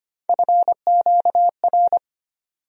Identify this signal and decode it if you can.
Morse code, heard as a keyed tone, about 25 wpm, 700 Hz. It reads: FQR
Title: USA Call Sign Postfixes - 3 Letters 25wpm